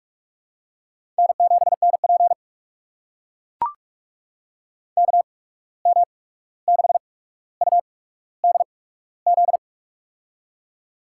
Morse code, heard as a keyed tone, about 44 wpm, spaced as for 17 wpm, 700 Hz. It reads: N7NP KM6UDZ
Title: USA Call Signs - 8x Character Spacing - 45wpm